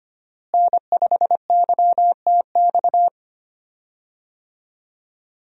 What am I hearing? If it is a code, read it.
Morse code, heard as a keyed tone, about 25 wpm, 700 Hz. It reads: N5YTX
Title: USA Call Signs - 5 Characters - Repeated 3x - 25wpm